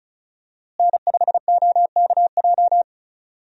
Morse code, heard as a keyed tone, about 36 wpm, 700 Hz. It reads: N5OKJ
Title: USA Call Signs - 5 Characters - Repeated 2x - 35wpm